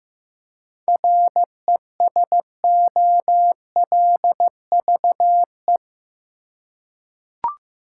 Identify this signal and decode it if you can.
Morse code, heard as a keyed tone, about 15 wpm, 700 Hz. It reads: RESOLVE
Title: Seven Letter Words - 15wpm